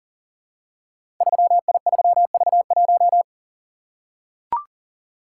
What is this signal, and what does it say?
Morse code, heard as a keyed tone, about 39 wpm, 700 Hz. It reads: 3I3V1
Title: Code Groups - 5 Letter:Numbers - 40wpm